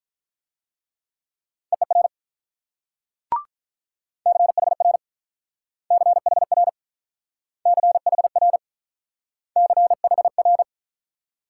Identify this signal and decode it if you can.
Morse code, heard as a keyed tone, about 45 wpm, 700 Hz. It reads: EER CHR CHR CHR CHR